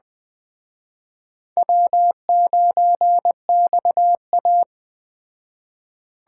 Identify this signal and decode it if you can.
Morse code, heard as a keyed tone, about 21 wpm, 700 Hz. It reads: W9XA
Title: Speed Racing - USA 1x2 Call Signs 17wpm